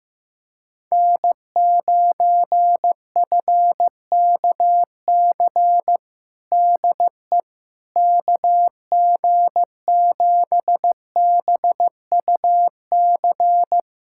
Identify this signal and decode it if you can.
Morse code, heard as a keyed tone, about 15 wpm, 700 Hz. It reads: N9FKC DE KG7BUC
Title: USA Callsign DE Callsign 15wpm